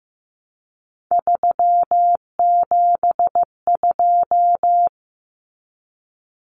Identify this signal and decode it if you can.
Morse code, heard as a keyed tone, about 15 wpm, 700 Hz. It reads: 372
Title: Common Cut Numbers - 3-Digit 15wpm